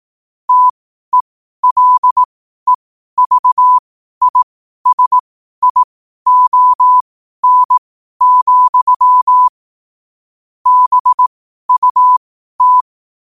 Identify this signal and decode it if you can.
Morse code, heard as a keyed tone, about 18 wpm, spaced as for 12 wpm, 1000 Hz. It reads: TELEVISION, BUT
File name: qotdfiles / 12wpm